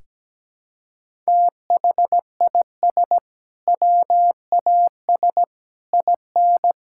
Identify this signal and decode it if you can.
Morse code, heard as a keyed tone, about 17 wpm, 700 Hz. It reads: THIS WAS IN
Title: Sentences from Top 1000 Words - Review Part 6 - 17wpm